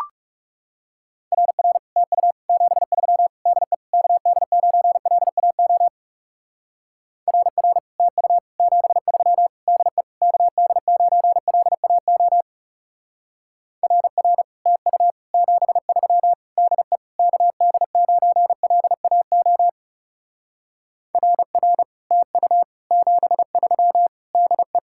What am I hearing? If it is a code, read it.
Morse code, heard as a keyed tone, about 36 wpm, 700 Hz. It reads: RR TU 73 DE KD9LAO RR TU 73 DE KD9LAO RR TU 73 DE KD9LAO RR TU 73 DE
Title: Speed-Racing Common QSO Phrases - 30wpm